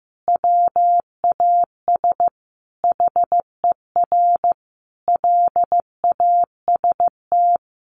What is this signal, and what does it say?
Morse code, heard as a keyed tone, about 15 wpm, 700 Hz. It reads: WAS HER LAST
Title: Sentences from Top 1400 Words 15wpm